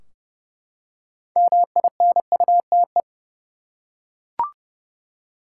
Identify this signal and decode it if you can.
Morse code, heard as a keyed tone, about 30 wpm, 700 Hz. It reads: MINUTE
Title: Six Letter Words - 30wpm